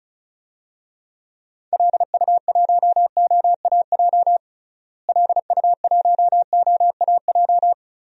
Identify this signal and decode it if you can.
Morse code, heard as a keyed tone, about 34 wpm, 700 Hz. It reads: LU1OAJ LU1OAJ
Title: DXCC-100 Call Signs - 6 Characters - Repeated 2x - 35wpm